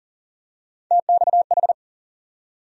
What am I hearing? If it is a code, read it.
Morse code, heard as a keyed tone, about 40 wpm, 700 Hz. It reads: TXH
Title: USA Call Sign Postfixes - 3 Letters 40wpm